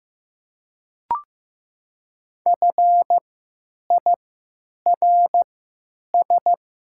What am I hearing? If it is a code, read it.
Morse code, heard as a keyed tone, about 15 wpm, spaced as for 9 wpm, 700 Hz. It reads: FIRS
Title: 3x Character Spacing - Top 100 Words - 15wpm